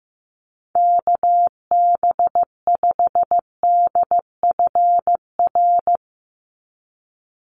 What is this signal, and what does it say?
Morse code, heard as a keyed tone, about 15 wpm, 700 Hz. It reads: KB5DFR
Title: USA Call Signs 15wpm